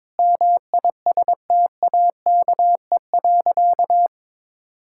Morse code, heard as a keyed tone, about 21 wpm, 700 Hz. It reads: MISTAKE.
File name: practice / 22wpm / Sentences with 4-Word Phrases - 22wpm